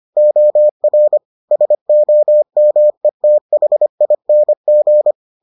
Morse code, heard as a keyed tone, about 25 wpm, 600 Hz. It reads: OR SOMETHING